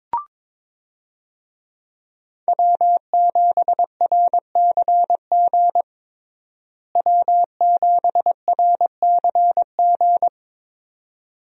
Morse code, heard as a keyed tone, about 22 wpm, 700 Hz. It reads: W7RCG W7RCG
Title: USA Call Signs - 5 Characters - Repeated 2x - 22wpm